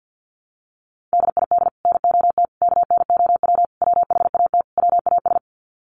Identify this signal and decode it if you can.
Morse code, heard as a keyed tone, about 50 wpm, 700 Hz. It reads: DID NOT KNOW WHAT WAS
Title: Sets of 5 Words Encore2 50wpm